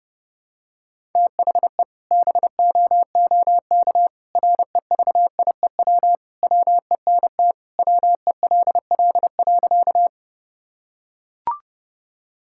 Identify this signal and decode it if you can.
Morse code, heard as a keyed tone, about 30 wpm, 700 Hz. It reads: THE BOOK REVIEW WENT WELL.
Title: Sentences from Top 1000 Words - Review Part 6 - 30wpm